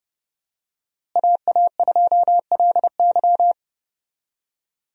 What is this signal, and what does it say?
Morse code, heard as a keyed tone, about 30 wpm, 700 Hz. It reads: AA2LY
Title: USA Call Signs - 5 Characters - Repeated 3x - 30wpm